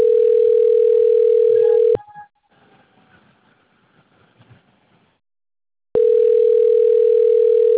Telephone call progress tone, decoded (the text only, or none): ringback tone